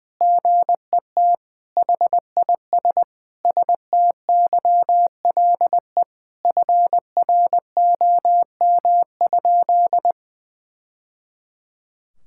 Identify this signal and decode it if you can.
Morse code, heard as a keyed tone, about 20 wpm, 700 Hz. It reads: GET HIS STYLE FROM?
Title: Sentences from Top 1000 Words - Review Part 5 - 20wpm